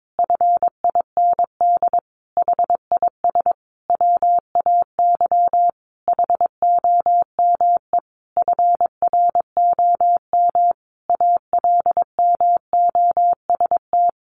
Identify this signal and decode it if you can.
Morse code, heard as a keyed tone, about 22 wpm, 700 Hz. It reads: FIND HIS WAY HOME FROM ALMOST
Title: Sentences from Top 1400 Words 22wpm